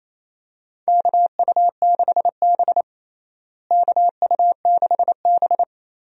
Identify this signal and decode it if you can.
Morse code, heard as a keyed tone, about 28 wpm, 700 Hz. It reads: KU6B KU6B
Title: USA Call Signs - 4 Characters - Repeated 3x - 28wpm